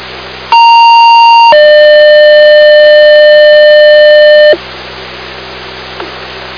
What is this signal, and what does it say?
Two-tone page, heard as a one-second tone, then a three-second tone, about 930 then 600 Hz. It is heard from 0.5 to 4.5 s.